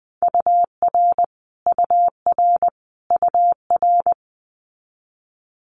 Morse code, heard as a keyed tone, about 20 wpm, 700 Hz. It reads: UR UR UR
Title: Abbreviations Repeated 3x - 20wpm